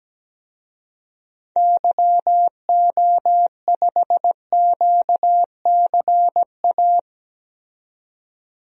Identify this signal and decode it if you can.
Morse code, heard as a keyed tone, about 17 wpm, 700 Hz. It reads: YO5QCA
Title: DXCC-100 Call Signs - 6 Characters - 17wpm